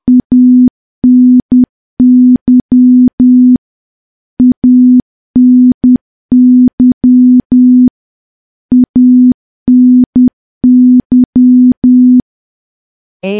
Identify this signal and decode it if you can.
Morse code, heard as a keyed tone, about 10 wpm, 250 Hz. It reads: ANY ANY ANY
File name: morse / morse-training-10WPM